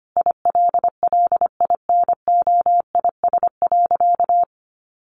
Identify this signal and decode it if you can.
Morse code, heard as a keyed tone, about 25 wpm, 700 Hz. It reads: ILLINOIS.
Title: A Sound Of Thunder 25wpm